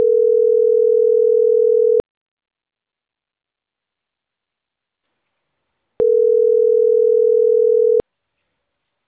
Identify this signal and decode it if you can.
Ringback tone (North America), 440 + 480 Hz.